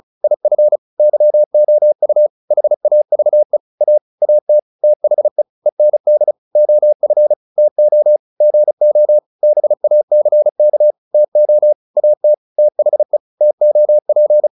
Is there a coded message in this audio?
Yes: Morse code from 0.2 to 14.5 s, heard as a keyed tone, about 34 wpm, 600 Hz. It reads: IF YOU HAVE A AT THE END OF TO GO BACK TO AT THE TOP